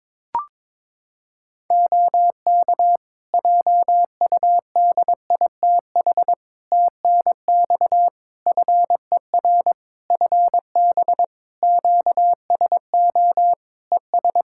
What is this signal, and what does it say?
Morse code, heard as a keyed tone, about 22 wpm, 700 Hz. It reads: OK JUDITH TNX FER FB QSO ES
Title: Practice QSO - Exchange 4 - Ending - 22wpm